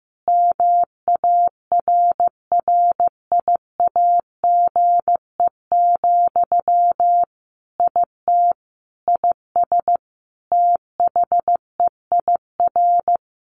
Morse code, heard as a keyed tone, about 14 wpm, 700 Hz. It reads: MARRIAGE, IT IS THEIR
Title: Sentences from Top 1500 Words 15wpm